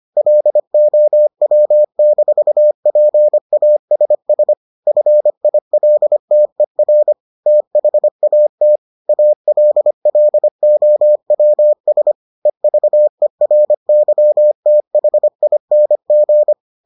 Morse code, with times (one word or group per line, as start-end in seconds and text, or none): 0.2-4.5 LOW-PASS
4.9-7.1 FILTER
7.5-8.8 THAT
9.1-12.1 ALLOWS
12.5-16.5 EVERYTHING